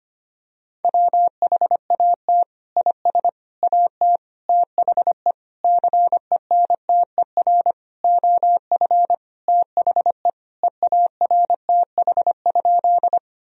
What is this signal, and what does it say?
Morse code, heard as a keyed tone, about 25 wpm, 700 Hz. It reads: WHAT IS AT THE CENTER OF THE EARTH?
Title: Sentences from Top 1000 Words - Review Part 2 - 25wpm